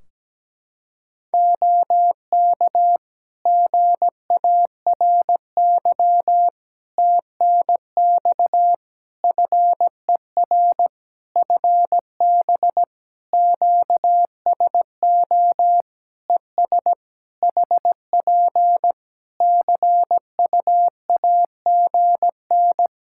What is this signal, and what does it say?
Morse code, heard as a keyed tone, about 17 wpm, 700 Hz. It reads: OK GARY TNX FER FB QSO ES HP CUAGN
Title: Practice QSO - Exchange 4 - Ending - 17wpm